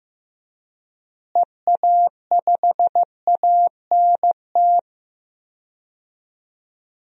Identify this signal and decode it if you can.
Morse code, heard as a keyed tone, about 15 wpm, 700 Hz. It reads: EA5ANT
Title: DXCC-100 Call Signs - 6 Characters - Repeated 2x - 15wpm